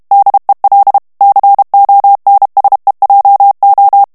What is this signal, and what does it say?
Morse code, heard as a keyed tone, about 32 wpm, 800 Hz. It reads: DEL CONSEJO
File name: morse mp3 / ESPAGNOL TEXTE EN CLAIR 1 VITESSE 30 MOTS MINUTES